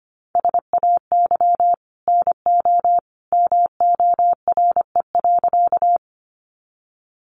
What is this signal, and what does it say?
Morse code, heard as a keyed tone, about 25 wpm, 700 Hz. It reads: SAY NO MORE.
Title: Sentences from Top 200 Words 25wpm